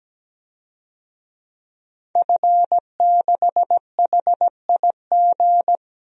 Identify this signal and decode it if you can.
Morse code, heard as a keyed tone, about 17 wpm, 700 Hz. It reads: F6HIG
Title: DXCC-100 Call Signs - 5 Characters - 17wpm